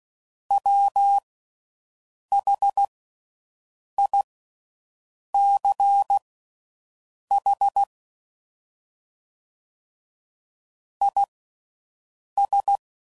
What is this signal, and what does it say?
Morse code, heard as a keyed tone, about 16 wpm, spaced as for 6 wpm, 800 Hz. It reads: WHICH IS